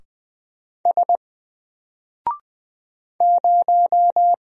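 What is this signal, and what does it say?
Morse code, heard as a keyed tone, about 20 wpm, 700 Hz. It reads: S 0
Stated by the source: Single letter and number Morse drill at 20 wpm, from N4LCD.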